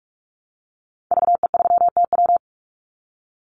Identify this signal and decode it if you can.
Morse code, heard as a keyed tone, about 46 wpm, 700 Hz. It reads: VE3TW
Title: Call Signs - 45wpm